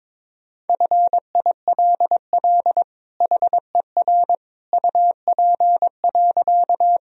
Morse code, heard as a keyed tone, about 22 wpm, 700 Hz. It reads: FILL HER UP.
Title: Sentences from Top 1000 Words - Review Part 3 - 22wpm